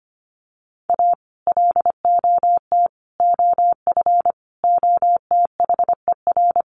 Morse code, heard as a keyed tone, about 25 wpm, 700 Hz. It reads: A LOT OF OTHER
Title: Sets of 5 Words Encore2 25wpm